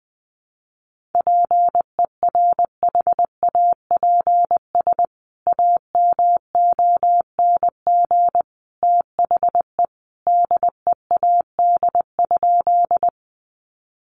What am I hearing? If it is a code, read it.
Morse code, heard as a keyed tone, about 20 wpm, 700 Hz. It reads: PERHAPS AMONG THE DEAD?
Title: Sentences from Top 1500 Words 20wpm